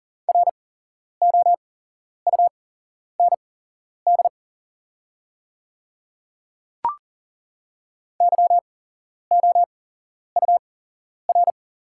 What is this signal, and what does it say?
Morse code, heard as a keyed tone, about 40 wpm, spaced as for 11 wpm, 690 Hz. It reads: ROUND YOUR
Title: 8x Character Spacing - Top 200 Words - 40wpm